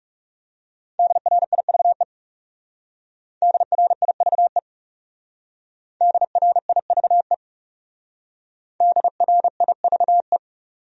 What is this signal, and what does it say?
Morse code, heard as a keyed tone, about 40 wpm, 700 Hz. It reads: DRIVE DRIVE DRIVE DRIVE